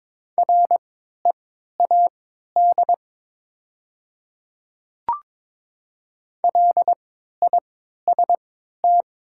Morse code, heard as a keyed tone, about 22 wpm, spaced as for 13 wpm, 700 Hz. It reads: READ LIST
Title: 3x Character Spacing - Top 500 Words - 22wpm